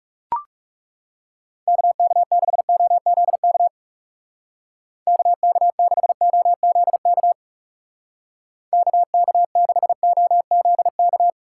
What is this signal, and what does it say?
Morse code, heard as a keyed tone, about 36 wpm, 700 Hz. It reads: KK6OZK KK6OZK KK6OZK
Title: Speed Racing - USA 2x3 Call Signs 30wpm